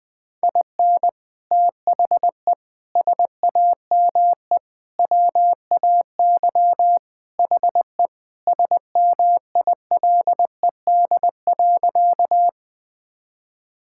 Morse code, heard as a keyed tone, about 20 wpm, 700 Hz. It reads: IN THE SAME WAY HE SMILED.
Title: Sentences with 4-Word Phrases - 20wpm